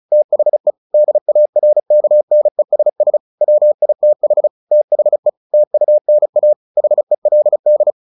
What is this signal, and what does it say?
Morse code, heard as a keyed tone, about 34 wpm, 600 Hz. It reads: THE DARKNESS WITH THE TUNA HELD
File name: the old man and the sea / the old man and the sea 1 35wpm